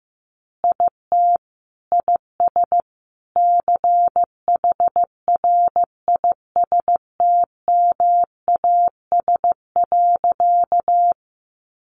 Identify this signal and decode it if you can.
Morse code, heard as a keyed tone, about 15 wpm, 700 Hz. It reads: IT IS CHRISTMAS.
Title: Easy Phrases 15wpm